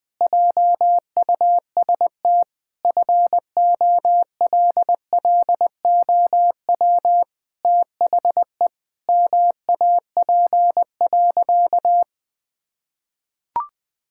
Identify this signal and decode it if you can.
Morse code, heard as a keyed tone, about 20 wpm, 700 Hz. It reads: JUST FOLLOW THE MAP.
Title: Sentences with 8-Word Phrases - 20wpm